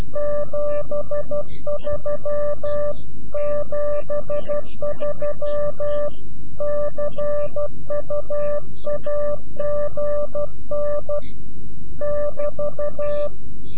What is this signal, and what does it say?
Morse code, heard as a keyed tone, about 12 wpm, 590 Hz. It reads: 73 73 CUAGN <BT>
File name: CW-QSO / CQ Call / CQ-QSO